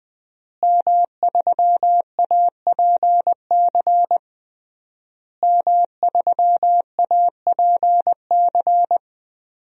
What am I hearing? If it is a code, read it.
Morse code, heard as a keyed tone, about 21 wpm, 700 Hz. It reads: M3APC M3APC
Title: DXCC-100 Call Signs - 5 Characters - Repeated 3x - 20wpm